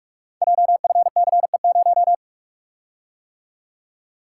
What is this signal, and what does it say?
Morse code, heard as a keyed tone, about 44 wpm, 690 Hz. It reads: JFCE0
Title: Code Groups - 5 Letter:Numbers - 45wpm